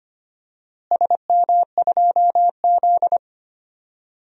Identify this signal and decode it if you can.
Morse code, heard as a keyed tone, about 25 wpm, 700 Hz. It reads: SM2Z
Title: DXCC-100 Call Signs - 4 Characters - Repeated 3x - 25wpm